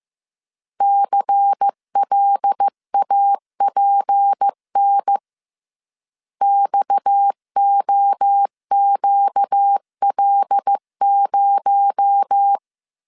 Morse code, heard as a keyed tone, about 15 wpm, 800 Hz. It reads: CLAPN XOQL0